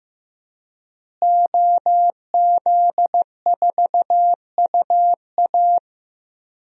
Morse code, heard as a keyed tone, about 15 wpm, 700 Hz. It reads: OZ4UA
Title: DXCC-100 Call Signs - 5 Characters - Repeated 2x - 15wpm